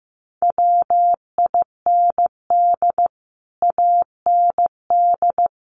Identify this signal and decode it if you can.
Morse code, heard as a keyed tone, about 15 wpm, 700 Hz. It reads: WIND AND
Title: Sentences from Top 1400 Words 15wpm